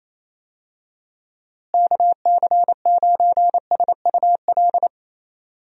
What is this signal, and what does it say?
Morse code, heard as a keyed tone, about 27 wpm, 700 Hz. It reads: KC9SUL
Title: USA Call Signs - 6 Characters - Repeated 2x - 28wpm